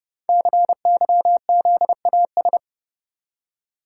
Morse code, heard as a keyed tone, about 29 wpm, 690 Hz. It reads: CYZAS
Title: Code Groups - 5 Letters - 30wpm